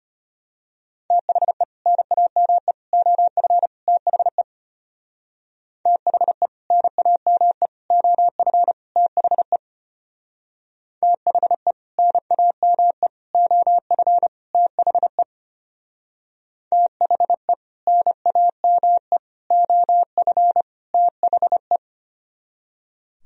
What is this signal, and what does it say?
Morse code, heard as a keyed tone, about 30 wpm, 700 Hz. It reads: THE NAME OF THE THE NAME OF THE THE NAME OF THE THE NAME OF THE
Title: Speed-Racing - Sets of 4 Words 25wpm